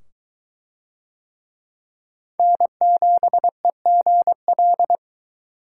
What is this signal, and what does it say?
Morse code, heard as a keyed tone, about 20 wpm, 700 Hz. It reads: N7EGL